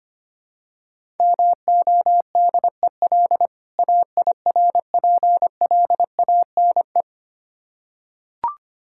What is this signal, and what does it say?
Morse code, heard as a keyed tone, about 25 wpm, 700 Hz. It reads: MODEL AIRPLANE
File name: practice / 25wpm / Difficult Words with Hints - 25wpm